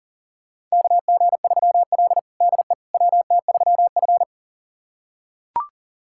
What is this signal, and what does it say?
Morse code, heard as a keyed tone, about 38 wpm, 690 Hz. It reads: KG3L DE WT3F
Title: USA Callsign DE Callsign - 4 Characters - 40wpm